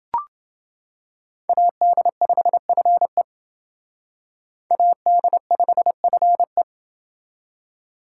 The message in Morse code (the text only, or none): AD5FE AD5FE